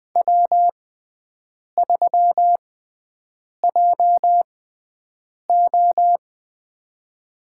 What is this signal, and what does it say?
Morse code, heard as a keyed tone, about 20 wpm, spaced as for 7 wpm, 700 Hz. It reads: W3JO
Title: USA Call Signs - 6x Character Spacing - 20wpm